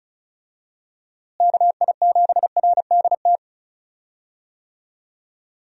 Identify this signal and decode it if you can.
Morse code, heard as a keyed tone, about 36 wpm, 700 Hz. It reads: KI7RDT
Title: USA Call Signs - 6 Characters - Repeated 2x - 35wpm